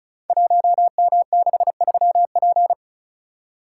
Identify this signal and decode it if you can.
Morse code, heard as a keyed tone, about 36 wpm, 700 Hz. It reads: 1M63P
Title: Code Groups - 5 Letter:Numbers - 35wpm